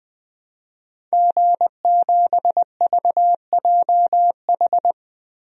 Morse code, heard as a keyed tone, about 20 wpm, 700 Hz. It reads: G7VJH
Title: DXCC-100 Call Signs - 5 Characters - Repeated 3x - 20wpm